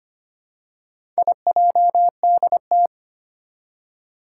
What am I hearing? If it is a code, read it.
Morse code, heard as a keyed tone, about 25 wpm, 700 Hz. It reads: IJDT
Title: Four Challenging Letter-Number Combinations 25wpm